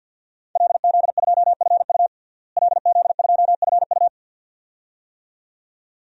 Morse code, heard as a keyed tone, about 50 wpm, 700 Hz. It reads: LZ2FU LZ2FU
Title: DXCC-100 Call Signs - 5 Characters - Repeated 2x - 50wpm